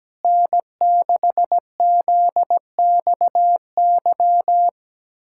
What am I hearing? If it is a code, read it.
Morse code, heard as a keyed tone, about 17 wpm, 700 Hz. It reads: N6ZXY
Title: USA Callsign DE Callsign 17wpm